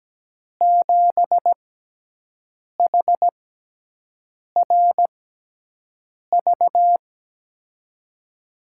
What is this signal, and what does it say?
Morse code, heard as a keyed tone, about 17 wpm, spaced as for 6 wpm, 700 Hz. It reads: 7HRV
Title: USA Call Signs - 6 Characters - 6x Character Spacing - 17wpm